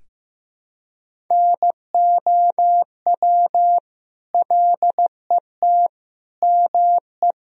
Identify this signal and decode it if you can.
Morse code, heard as a keyed tone, about 15 wpm, 700 Hz. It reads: NOW LET ME
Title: Sentences from Top 300 Words and Abbreviations 15wpm